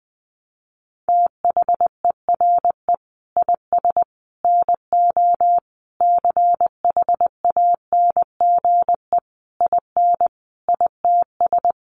Morse code, heard as a keyed tone, about 20 wpm, 700 Hz. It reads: THERE IS NO CHANGE IN ITS
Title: Sentences from Top 1300 Words 20wpm